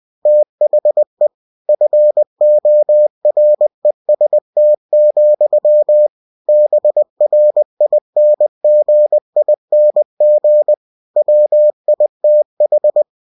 Morse code, heard as a keyed tone, about 20 wpm, 600 Hz. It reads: THE FOREST, BRINGING WITH
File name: cw-books / winnie-the-pooh / winnie-20wpm-page-0101